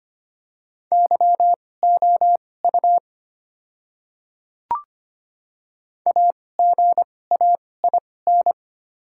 Morse code, heard as a keyed tone, about 25 wpm, spaced as for 18 wpm, 700 Hz. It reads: YOU AGAIN
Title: 2x Character Spacing - Top 200 Words - 25wpm